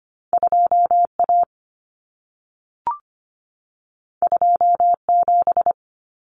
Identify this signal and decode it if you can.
Morse code, heard as a keyed tone, about 25 wpm, 700 Hz. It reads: 2A 27
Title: Common Cut Numbers - 2-Digit 25wpm